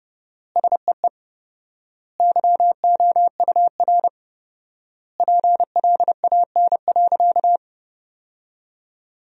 Morse code, heard as a keyed tone, about 30 wpm, 700 Hz. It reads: SEE YOUR PLAN.
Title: 4x Word-Spacing - Sentences from Top 500 Words - 30wpm